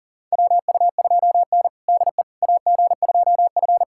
Morse code, heard as a keyed tone, about 40 wpm, 700 Hz. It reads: WU2N DE AG2F
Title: USA Callsign DE Callsign - 4 Characters - 40wpm